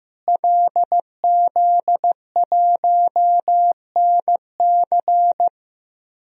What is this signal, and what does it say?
Morse code, heard as a keyed tone, about 15 wpm, 700 Hz. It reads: LZ1NC
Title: DXCC-100 Call Signs - 5 Characters - Repeated 2x - 15wpm